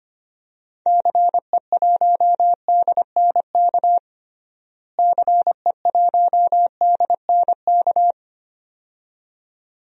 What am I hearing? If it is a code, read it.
Morse code, heard as a keyed tone, about 25 wpm, 700 Hz. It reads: CE1DNK CE1DNK
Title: DXCC-100 Call Signs - 6 Characters - Repeated 2x - 25wpm